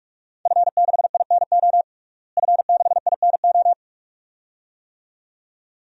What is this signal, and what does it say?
Morse code, heard as a keyed tone, about 46 wpm, 700 Hz. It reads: F6INO F6INO
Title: DXCC-100 Call Signs - 5 Characters - Repeated 2x - 45wpm